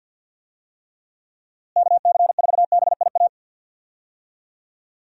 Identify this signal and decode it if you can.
Morse code, heard as a keyed tone, about 50 wpm, 700 Hz. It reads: KC4BIA
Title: USA Call Signs - 6 Characters - Repeated 2x - 50wpm